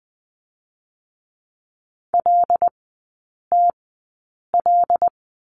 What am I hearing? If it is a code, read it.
Morse code, heard as a keyed tone, about 20 wpm, spaced as for 8 wpm, 700 Hz. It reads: LTL
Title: 121 Lesson 20e - Number 3 - Three Characters 20wpm